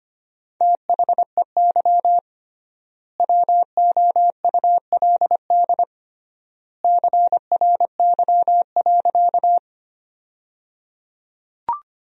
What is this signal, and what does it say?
Morse code, heard as a keyed tone, about 25 wpm, 700 Hz. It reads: THEY WOULD CRY.
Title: 3x Word-Spacing - Sentences from Top 500 Words - 25wpm